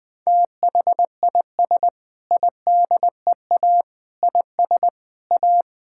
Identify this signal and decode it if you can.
Morse code, heard as a keyed tone, about 20 wpm, 700 Hz. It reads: THIS IDEA IS A
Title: Sentences with 8-Word Phrases - Spoken Beginning - 20wpm